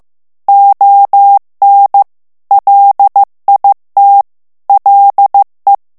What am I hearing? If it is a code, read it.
Morse code, heard as a keyed tone, about 15 wpm, 800 Hz. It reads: ON LIT LE
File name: morse mp3 / FRANCAIS TEXTE CLAIR 1 VITESSE 14 MOTS MINUTES